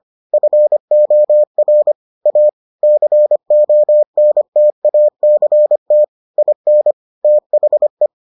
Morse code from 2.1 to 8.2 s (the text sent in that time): A CONTACT IN THE